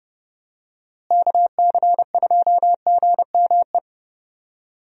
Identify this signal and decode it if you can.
Morse code, heard as a keyed tone, about 30 wpm, 700 Hz. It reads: KC2GME
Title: USA Call Signs - 6 Characters - Repeated 2x - 30wpm